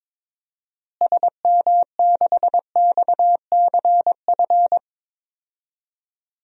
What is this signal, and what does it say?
Morse code, heard as a keyed tone, about 22 wpm, 700 Hz. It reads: SM6XCF
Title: DXCC-100 Call Signs - 6 Characters - Repeated 3x - 22wpm